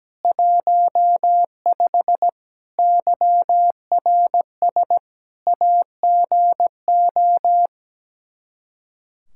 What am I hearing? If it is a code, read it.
Morse code, heard as a keyed tone, about 17 wpm, 700 Hz. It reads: 15 YRS AGO
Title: K1USN SST QSO Practice 17wpm